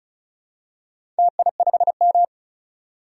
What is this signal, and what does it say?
Morse code, heard as a keyed tone, about 36 wpm, 700 Hz. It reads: TI5M
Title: DXCC-100 Call Signs - 4 Characters - 35wpm